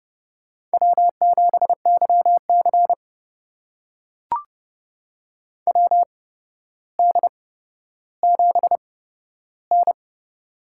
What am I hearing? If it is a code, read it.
Morse code, heard as a keyed tone, about 30 wpm, spaced as for 13 wpm, 700 Hz. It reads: W7YC WD7N
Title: USA Call Signs - 4 Characters - 8x Character Spacing - 30wpm